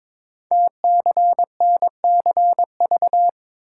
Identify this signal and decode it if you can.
Morse code, heard as a keyed tone, about 22 wpm, 700 Hz. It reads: TCNCV
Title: Code Groups - 5 Letters - 22wpm